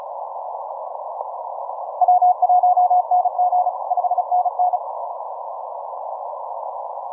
Morse code, heard as a keyed tone, about 35 wpm, 720 Hz. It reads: W1NG 5NN
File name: simple-file-list / 6O6O 15MCW